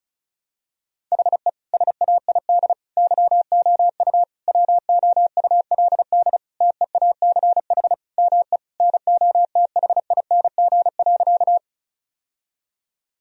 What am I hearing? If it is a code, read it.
Morse code, heard as a keyed tone, about 34 wpm, 700 Hz. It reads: HE SAID YOU WOULD TEACH ME NOTHING.
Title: Sentences from Top 1000 Words - Review Part 2 - 35wpm